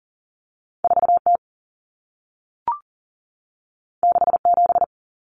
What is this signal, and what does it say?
Morse code, heard as a keyed tone, about 40 wpm, 700 Hz. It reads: 4T 67
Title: Common Cut Numbers - 2-Digit 40wpm